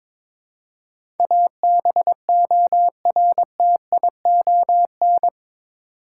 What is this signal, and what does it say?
Morse code, heard as a keyed tone, about 21 wpm, 700 Hz. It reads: ABORTION
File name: practice / 22wpm / Eight Letter Words - 22wpm